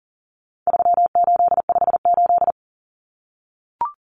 Morse code, heard as a keyed tone, about 40 wpm, 700 Hz. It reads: 3858
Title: Common Cut Numbers - 4-Digit 40wpm